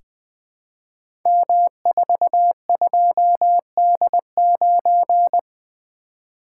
Morse code, heard as a keyed tone, about 20 wpm, 700 Hz. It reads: M42D9
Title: Code Groups - 5 Letter:Numbers - 20wpm